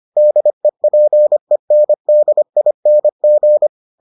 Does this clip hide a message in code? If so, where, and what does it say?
Yes: Morse code from 0.2 to 3.7 s, heard as a keyed tone, about 25 wpm, 600 Hz. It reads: DEPENDING